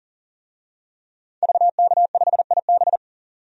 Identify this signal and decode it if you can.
Morse code, heard as a keyed tone, about 40 wpm, 700 Hz. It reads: VK5IB
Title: DXCC-100 Call Signs - 5 Characters - 40wpm